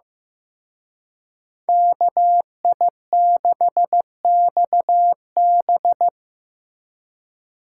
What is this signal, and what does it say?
Morse code, heard as a keyed tone, about 18 wpm, 700 Hz. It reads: KI6XB
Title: Speed Racing - USA 2x2 Call Signs 15wpm